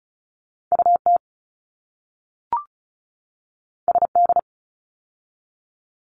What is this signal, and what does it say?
Morse code, heard as a keyed tone, about 34 wpm, 700 Hz. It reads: UT SD
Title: ARRL Sections 35wpm